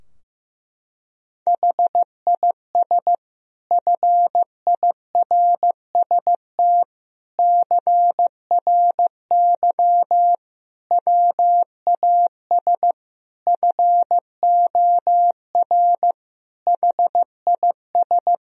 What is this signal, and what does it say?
Morse code, heard as a keyed tone, about 15 wpm, 700 Hz. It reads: HIS FIRST CRY WAS FOR HIS
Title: Sentences from Top 1000 Words - Review Part 2 - 15wpm